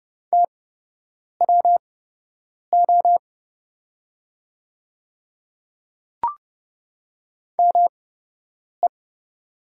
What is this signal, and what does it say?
Morse code, heard as a keyed tone, about 30 wpm, spaced as for 8 wpm, 700 Hz. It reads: TWO ME
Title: 8x Character Spacing - Top 300 Words - 30wpm